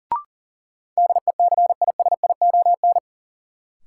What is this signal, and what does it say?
Morse code, heard as a keyed tone, about 38 wpm, 700 Hz. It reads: DECISION
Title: Sentences from Top 1000 Words - New Word Drill - 40wpm